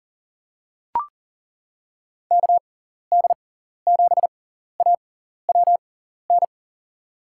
Morse code, heard as a keyed tone, about 40 wpm, spaced as for 14 wpm, 700 Hz. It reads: KD7AWN
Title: USA Call Signs - 6 Characters - 6x Character Spacing - 40wpm